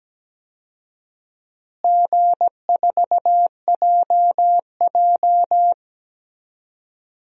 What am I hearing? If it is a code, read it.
Morse code, heard as a keyed tone, about 17 wpm, 700 Hz. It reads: G4JJ
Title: DXCC-100 Call Signs - 4 Characters - Repeated 3x - 17wpm